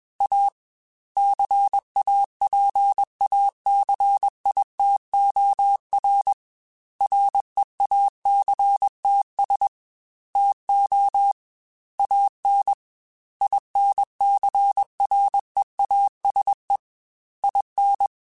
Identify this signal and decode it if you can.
Morse code, heard as a keyed tone, about 21 wpm, 800 Hz. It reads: A CAPACITOR REACTS TO AN INCREASE IN
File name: Technical Subjects / Technical Subjects-mp3s / XC @20